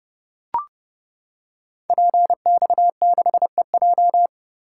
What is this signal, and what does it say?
Morse code, heard as a keyed tone, about 29 wpm, 700 Hz. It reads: PX6EJ
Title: Code Groups - 5 Letter:Numbers - 30wpm